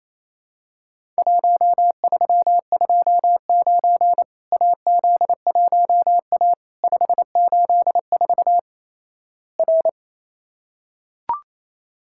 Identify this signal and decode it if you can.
Morse code, heard as a keyed tone, about 28 wpm, 700 Hz. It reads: 1329 AZ1A 584 R
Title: Worked All Europe DX Contest - QTC Exchange - 28wpm